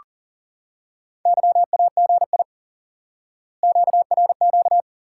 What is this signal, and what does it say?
Morse code, heard as a keyed tone, about 40 wpm, 700 Hz. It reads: YAGI QRQ
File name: IQE Course 3 - Lesson 89 - YAGI - Two Elements - 40wpm